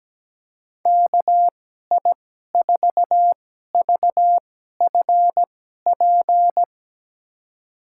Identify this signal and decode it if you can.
Morse code, heard as a keyed tone, about 17 wpm, spaced as for 12 wpm, 700 Hz. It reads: KI4VFP
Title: USA Call Signs - 2x Character Spacing - 17wpm